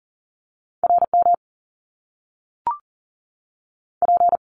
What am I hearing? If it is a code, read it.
Morse code, heard as a keyed tone, about 39 wpm, 700 Hz. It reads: RM P
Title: Top Two Letter Combinations in Words 40wpm